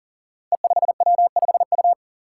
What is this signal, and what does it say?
Morse code, heard as a keyed tone, about 40 wpm, 700 Hz. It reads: E5W5U
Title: Five Challenging Letter-Number Combinations 40wpm